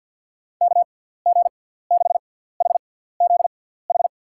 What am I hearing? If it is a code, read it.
Morse code, heard as a keyed tone, about 49 wpm, spaced as for 17 wpm, 700 Hz. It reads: KG6HZH
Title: USA Call Signs - 6 Characters - 6x Character Spacing - 50wpm